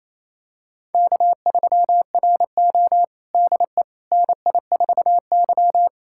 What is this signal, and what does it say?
Morse code, heard as a keyed tone, about 28 wpm, 700 Hz. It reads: K3RO DE NI4Y
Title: USA Callsign DE Callsign - 4 Characters - 28wpm